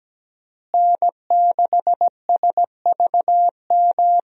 Morse code, heard as a keyed tone, about 17 wpm, 700 Hz. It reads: N6SVM
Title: USA Call Signs - 5 Characters - Repeated 3x - 17wpm